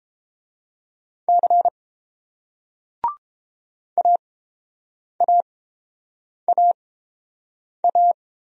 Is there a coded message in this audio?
Yes: Morse code from 1.3 to 8.1 s, heard as a keyed tone, about 30 wpm, spaced as for 7 wpm, 690 Hz. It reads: C AAAA